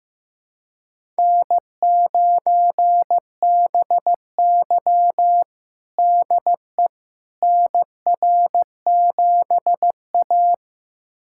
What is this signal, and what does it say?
Morse code, heard as a keyed tone, about 15 wpm, 700 Hz. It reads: N9BY DE NR7A
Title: USA Callsign DE Callsign - 4 Characters - 15wpm